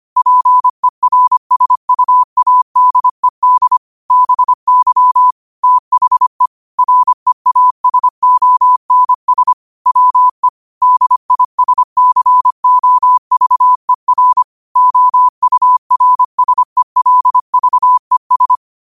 Morse code, heard as a keyed tone, about 25 wpm, 1000 Hz. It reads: PERSUADED BY THE REASONS WE DISCOVER OURSELVES